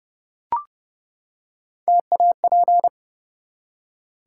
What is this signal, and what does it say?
Morse code, heard as a keyed tone, about 28 wpm, 700 Hz. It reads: TAP
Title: Three Letter Words - 30wpm